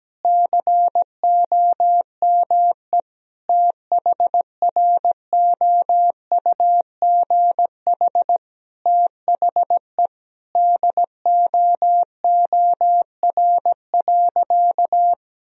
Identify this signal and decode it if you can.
Morse code, heard as a keyed tone, about 17 wpm, 700 Hz. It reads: COME THROUGH THE DOOR.
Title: Sentences from Top 1000 Words - Review Part 1 - 17wpm